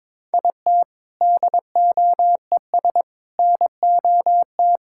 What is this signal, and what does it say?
Morse code, heard as a keyed tone, about 22 wpm, 700 Hz. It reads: IT DOES NOT
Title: Sentences from Top 1600 Words 22wpm